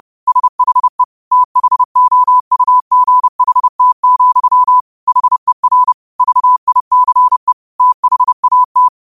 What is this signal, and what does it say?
Morse code, heard as a keyed tone, about 30 wpm, 1000 Hz. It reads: SHE THOUGHT, HER VICE THAT
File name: qotdfiles / 30wpm